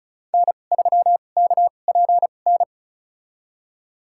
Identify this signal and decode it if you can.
Morse code, heard as a keyed tone, about 36 wpm, spaced as for 25 wpm, 700 Hz. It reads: N3KPN
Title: USA Call Signs - 2x Character Spacing - 35wpm